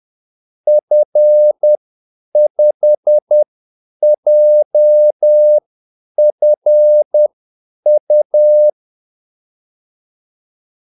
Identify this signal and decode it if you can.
Morse code, heard as a keyed tone, about 10 wpm, spaced as for 8 wpm, 600 Hz. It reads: F5JFU
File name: Callsigns-10-8wpm